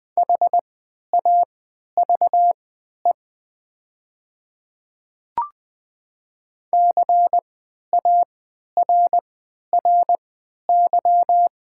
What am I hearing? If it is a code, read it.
Morse code, heard as a keyed tone, about 20 wpm, spaced as for 11 wpm, 700 Hz. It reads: HAVE CARRY
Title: 3x Character Spacing - Top 500 Words - 20wpm